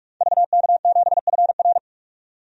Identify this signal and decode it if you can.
Morse code, heard as a keyed tone, about 46 wpm, 700 Hz. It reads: VK7FR
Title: DXCC-100 Call Signs - 5 Characters - 45wpm